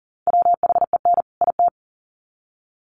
Morse code, heard as a keyed tone, about 40 wpm, 700 Hz. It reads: WHEN IT
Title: Sets of 2 Words 40wpm